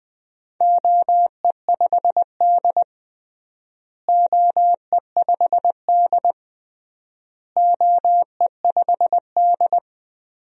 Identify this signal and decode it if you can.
Morse code, heard as a keyed tone, about 20 wpm, 700 Hz. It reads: OE5D OE5D OE5D
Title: DXCC-100 Call Signs - 4 Characters - Repeated 3x - 20wpm